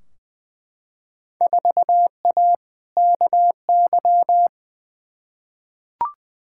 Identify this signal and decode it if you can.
Morse code, heard as a keyed tone, about 20 wpm, 700 Hz. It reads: 4A KY
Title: Field Day Exchanges Encore - 20wpm